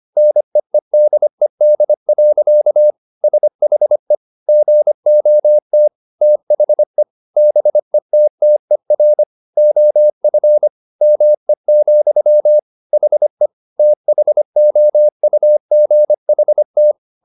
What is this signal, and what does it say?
Morse code, heard as a keyed tone, about 25 wpm, 600 Hz. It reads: NEEDED. SHE GOT THE BETTER OF ME, HE THOUGHT